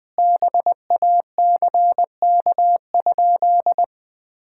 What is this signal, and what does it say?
Morse code, heard as a keyed tone, about 21 wpm, 700 Hz. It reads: BACK?
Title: Sentences from Top 1600 Words 20wpm